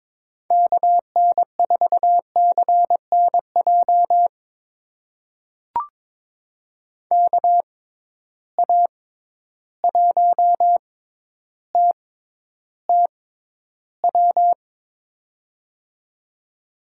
Morse code, heard as a keyed tone, about 22 wpm, spaced as for 11 wpm, 700 Hz. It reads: KN4CNJ KA1TTW
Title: USA Call Signs - 6 Characters - 6x Character Spacing - 22wpm